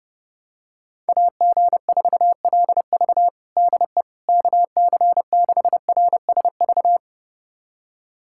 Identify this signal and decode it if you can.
Morse code, heard as a keyed tone, about 30 wpm, 700 Hz. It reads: AG4LV DE KC6RSV
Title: USA Callsign DE Callsign 30wpm